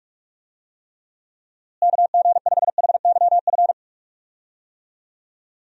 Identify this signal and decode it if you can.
Morse code, heard as a keyed tone, about 45 wpm, 700 Hz. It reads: KG5HYF
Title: USA Call Signs - 6 Characters - Repeated 3x - 45wpm